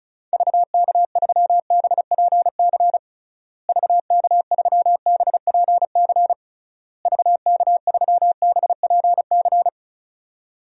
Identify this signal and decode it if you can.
Morse code, heard as a keyed tone, about 36 wpm, 700 Hz. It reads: VK3BPC VK3BPC VK3BPC
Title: DXCC-100 Call Signs - 6 Characters - Repeated 3x - 35wpm